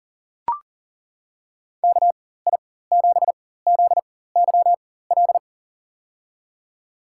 Morse code, heard as a keyed tone, about 39 wpm, spaced as for 18 wpm, 700 Hz. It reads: KI7ZYL
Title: USA Call Signs - 6 Characters - 4x Character Spacing - 40wpm